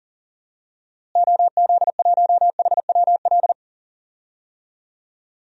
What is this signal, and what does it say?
Morse code, heard as a keyed tone, about 40 wpm, 700 Hz. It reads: OZ1HWL
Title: DXCC-100 Call Signs - 6 Characters - Repeated 3x - 40wpm